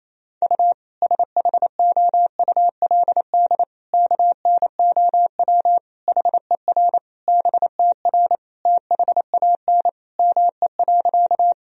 Morse code, heard as a keyed tone, about 27 wpm, 700 Hz. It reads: U SHOULD KNOW HER BTR THAN ME.
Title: Sentences from Top 400 Words and Abbreviations 28wpm